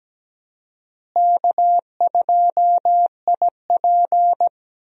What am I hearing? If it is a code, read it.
Morse code, heard as a keyed tone, about 17 wpm, 700 Hz. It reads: K2IP
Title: USA Call Signs - 4 Characters - Repeated 3x - 17wpm